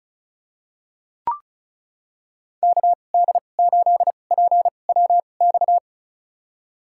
Morse code, heard as a keyed tone, about 34 wpm, spaced as for 25 wpm, 700 Hz. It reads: KD8PWX
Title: USA Call Signs - 6 Characters - 2x Character Spacing - 35wpm